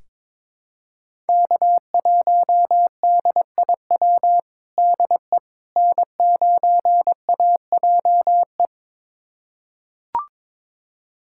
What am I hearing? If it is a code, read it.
Morse code, heard as a keyed tone, about 22 wpm, 700 Hz. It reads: K1DIW DE N9AJE
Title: USA Callsign DE Callsign - 5 Characters - 22wpm